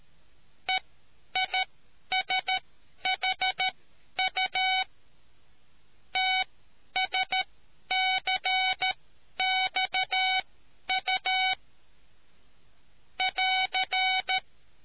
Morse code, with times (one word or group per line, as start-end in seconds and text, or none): 0.7-4.8 EISHU
6.2-11.5 TSCXU
13.2-14.4 <AR>